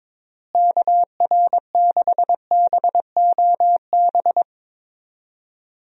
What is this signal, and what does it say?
Morse code, heard as a keyed tone, about 22 wpm, 700 Hz. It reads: KR6BOB
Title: USA Call Signs - 6 Characters - Repeated 3x - 22wpm